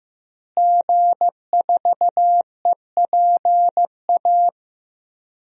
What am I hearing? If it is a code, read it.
Morse code, heard as a keyed tone, about 15 wpm, 700 Hz. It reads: G4EPA
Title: DXCC-100 Call Signs - 5 Characters - Repeated 2x - 15wpm